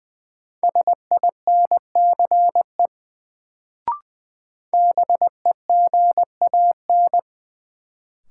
Morse code, heard as a keyed tone, about 20 wpm, 700 Hz. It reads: SINCE BEGAN
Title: Sentences from Top 300 Words - New Word Drill - 20wpm